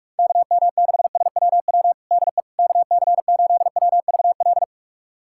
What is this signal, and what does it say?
Morse code, heard as a keyed tone, about 42 wpm, 690 Hz. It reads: KM6SWW DE KC8WVL
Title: USA Callsign DE Callsign 45wpm